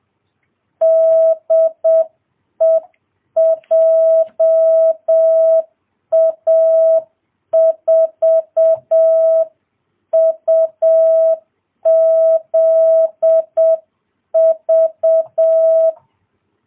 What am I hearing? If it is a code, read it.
Morse code, heard as a keyed tone, about 7 wpm, 650 Hz. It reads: DE JA4UZV